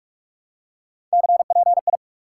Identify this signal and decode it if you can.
Morse code, heard as a keyed tone, about 42 wpm, 700 Hz. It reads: CPI
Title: Top 90 QSO Elements 45wpm